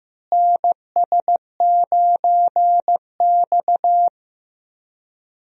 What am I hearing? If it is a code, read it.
Morse code, heard as a keyed tone, about 15 wpm, 700 Hz. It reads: NS9X
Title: USA Call Signs - 4 Characters - Repeated 2x - 15wpm